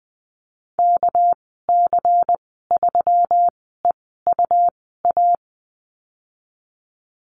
Morse code, heard as a keyed tone, about 20 wpm, spaced as for 14 wpm, 700 Hz. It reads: KC3EUA
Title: 238 Lesson 38g - Letter q - Call Signs 20wpm